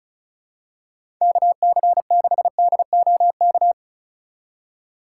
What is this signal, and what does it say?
Morse code, heard as a keyed tone, about 34 wpm, 700 Hz. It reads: KC6DOK
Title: USA Call Signs - 6 Characters - 2x Character Spacing - 35wpm